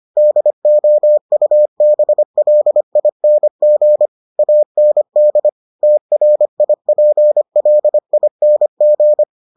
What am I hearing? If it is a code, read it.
Morse code, heard as a keyed tone, about 25 wpm, 600 Hz. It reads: DOUBLING AND TRIPLING